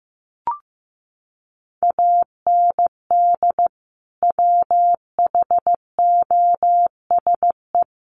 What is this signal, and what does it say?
Morse code, heard as a keyed tone, about 15 wpm, 700 Hz. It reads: AND WHOSE
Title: Sentences from Top 1200 Words 15wpm